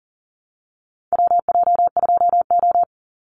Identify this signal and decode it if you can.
Morse code, heard as a keyed tone, about 40 wpm, 700 Hz. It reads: WJ2O
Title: Call Signs Repeated 3x - 40wpm